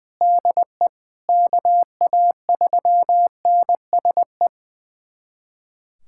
Morse code, heard as a keyed tone, about 20 wpm, 700 Hz. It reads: DE KA3NSE
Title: USA Callsign DE Callsign - 6 Characters - 20wpm